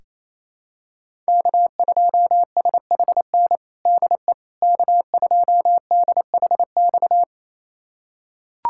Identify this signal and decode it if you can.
Morse code, heard as a keyed tone, about 28 wpm, 700 Hz. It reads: K2SHN DE K2DHX
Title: USA Callsign DE Callsign - 5 Characters - 28wpm